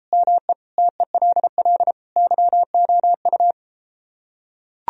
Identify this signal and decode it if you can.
Morse code, heard as a keyed tone, about 33 wpm, 700 Hz. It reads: ME TELL YOU